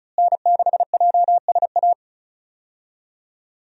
Morse code, heard as a keyed tone, about 36 wpm, 700 Hz. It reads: N6JSA
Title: USA Call Signs - 5 Characters - Repeated 3x - 35wpm